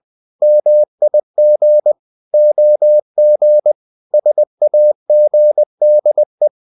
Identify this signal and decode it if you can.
Morse code, heard as a keyed tone, about 20 wpm, 600 Hz. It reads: MIG OG SAGDE
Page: Ferie 20 wpm